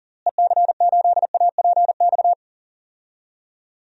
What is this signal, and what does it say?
Morse code, heard as a keyed tone, about 40 wpm, 700 Hz. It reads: EC8APX
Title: DXCC-100 Call Signs - 6 Characters - Repeated 3x - 40wpm